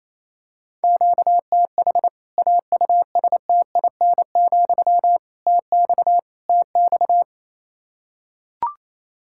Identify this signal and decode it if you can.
Morse code, heard as a keyed tone, about 28 wpm, 700 Hz. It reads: QTH AUSTIN, TX TX
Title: K1USN SST QSO Practice 28wpm